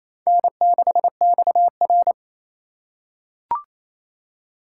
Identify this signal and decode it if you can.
Morse code, heard as a keyed tone, about 28 wpm, 700 Hz. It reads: N6XR
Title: USA Call Signs - 4 Characters - 3x Character Spacing - 28wpm